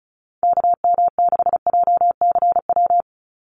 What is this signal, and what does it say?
Morse code, heard as a keyed tone, about 34 wpm, 700 Hz. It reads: KM6JCW
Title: USA Call Signs 35wpm